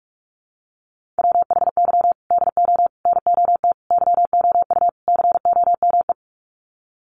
Morse code, heard as a keyed tone, about 44 wpm, 700 Hz. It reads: WHY DO NOT YOU COME
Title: Sets of 5 Words Encore2 45wpm